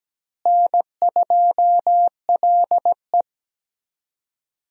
Morse code, heard as a keyed tone, about 17 wpm, 700 Hz. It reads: N2LE
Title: USA Call Signs - 4 Characters - Repeated 3x - 17wpm